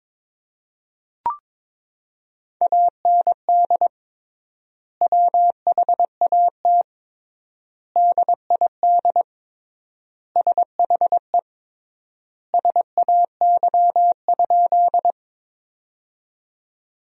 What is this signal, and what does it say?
Morse code, heard as a keyed tone, about 23 wpm, 700 Hz. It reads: AND WHAT DID SHE SAY?
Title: 3x Word-Spacing - Sentences from Top 200 Words - 22wpm